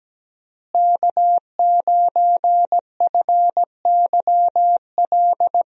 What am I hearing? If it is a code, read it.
Morse code, heard as a keyed tone, about 17 wpm, 700 Hz. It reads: K9FYL
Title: USA Call Signs - 5 Characters - Repeated 2x - 17wpm